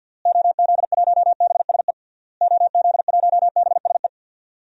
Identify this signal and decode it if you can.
Morse code, heard as a keyed tone, about 51 wpm, 700 Hz. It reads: OZ1BSE OZ1BSE
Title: DXCC-100 Call Signs - 6 Characters - Repeated 3x - 50wpm